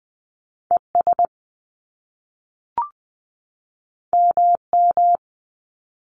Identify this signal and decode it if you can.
Morse code, heard as a keyed tone, about 20 wpm, 700 Hz. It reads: E S M M
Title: Top Two Letter Combinations in Words 20wpm